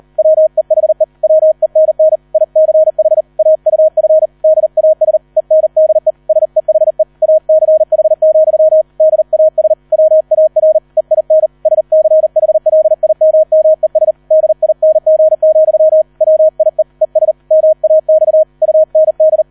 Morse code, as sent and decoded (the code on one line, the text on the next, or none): .-- . .... / .-- . -. -. / .. -.-. .... / .- ..- ..-. / -.. .- ... / . -. -.. / ... .... / .- -.-. .... --..-- / -.. .- ... / .-- .- .-. / . .. -. / ... -.-. .... .-.. .. -- -- . ... / -.. .. -. --. --..-- / .-- .. / . ... / -- .- -..- / ..- -. -..
WEHE WENN ICH AUF DAS ENDE SEHE ACH, DAS WAR EIN SCHLIMMES DING, WIE ES MAX UND